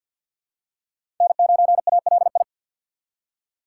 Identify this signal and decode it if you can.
Morse code, heard as a keyed tone, about 48 wpm, 700 Hz. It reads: N9ALI
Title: USA Call Signs - 5 Characters - 3x Character Spacing - 50wpm